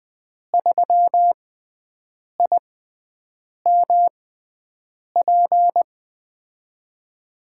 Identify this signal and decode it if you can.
Morse code, heard as a keyed tone, about 20 wpm, spaced as for 7 wpm, 700 Hz. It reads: 3IMP
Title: USA Call Signs - 6x Character Spacing - 20wpm